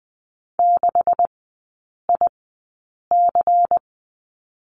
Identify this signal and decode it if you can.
Morse code, heard as a keyed tone, about 20 wpm, spaced as for 8 wpm, 700 Hz. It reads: 6IC
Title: 152 Lesson 25e - Letter p - Three Characters 20wpm